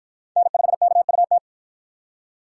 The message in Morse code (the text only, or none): N5KUT